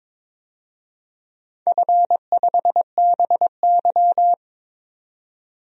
Morse code, heard as a keyed tone, about 22 wpm, 700 Hz. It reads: F5BY
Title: DXCC-100 Call Signs - 4 Characters - 22wpm